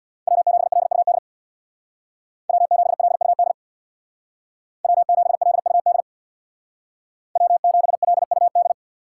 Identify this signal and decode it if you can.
Morse code, heard as a keyed tone, about 64 wpm, 700 Hz. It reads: W7LUD W7LUD W7LUD W7LUD